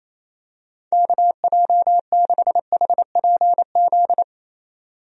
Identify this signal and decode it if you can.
Morse code, heard as a keyed tone, about 28 wpm, 700 Hz. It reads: KJ6HPZ
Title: USA Call Signs - 6 Characters - 28wpm